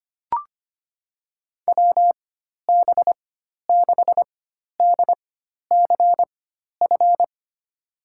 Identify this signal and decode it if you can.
Morse code, heard as a keyed tone, about 25 wpm, spaced as for 12 wpm, 700 Hz. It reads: WB6DCF
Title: USA Call Signs - 6 Characters - 4x Character Spacing - 25wpm